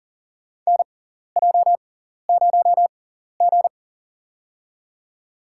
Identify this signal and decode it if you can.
Morse code, heard as a keyed tone, about 40 wpm, spaced as for 14 wpm, 700 Hz. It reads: NJ0G
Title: USA Call Signs - 4 Characters - 6x Character Spacing - 40wpm